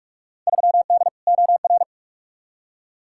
Morse code, heard as a keyed tone, about 46 wpm, 700 Hz. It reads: 3D OR
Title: Field Day Exchanges - 45wpm